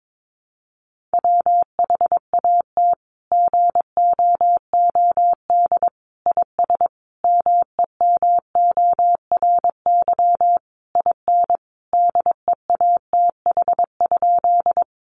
Morse code, heard as a keyed tone, about 22 wpm, 700 Hz. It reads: WHAT GOOD IS MEMORY IN DEATH?
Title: Sentences from Top 1300 Words 22wpm